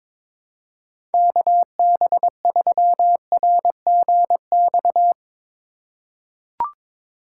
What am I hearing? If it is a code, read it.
Morse code, heard as a keyed tone, about 22 wpm, 700 Hz. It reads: KB3RGX
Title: USA Call Signs - 3x Character Spacing - 22wpm